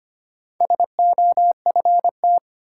Morse code, heard as a keyed tone, about 25 wpm, 700 Hz. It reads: SOFT
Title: Sentences from Top 1500 Words - New Word Drill Alt - 25wpm